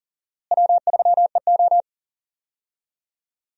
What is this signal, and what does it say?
Morse code, heard as a keyed tone, about 40 wpm, 700 Hz. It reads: W3EO
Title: USA Call Signs - 4 Characters - Repeated 3x - 40wpm